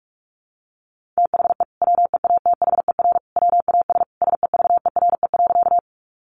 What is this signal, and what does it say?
Morse code, heard as a keyed tone, about 46 wpm, 700 Hz. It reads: THE WEATHER WAS SEVERE.
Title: Sentences from Top 1400 Words 45wpm